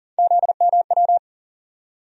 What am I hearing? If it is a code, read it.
Morse code, heard as a keyed tone, about 37 wpm, 700 Hz. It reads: ZMW
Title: USA Call Sign Postfixes - 3 Letters 40wpm